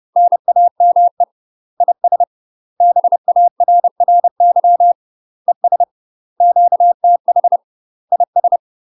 Morse code, heard as a keyed tone, about 30 wpm, 700 Hz. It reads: NAME IS BARRY ES QTH IS
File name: UK MASTER QSOs TRAINING/30WPM